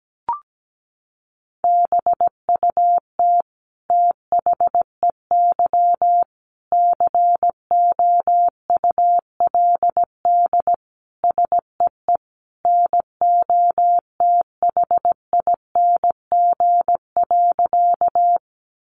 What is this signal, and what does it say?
Morse code, heard as a keyed tone, about 17 wpm, 700 Hz. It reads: BUT THEY COULD SEE NOTHING.
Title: Sentences from Top 500 Words 17wpm